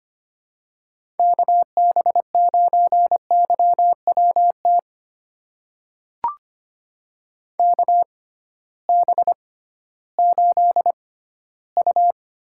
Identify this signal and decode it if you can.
Morse code, heard as a keyed tone, about 25 wpm, spaced as for 14 wpm, 700 Hz. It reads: KB9YWT KB8U
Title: USA Call Signs - 6 Characters - 6x Character Spacing - 25wpm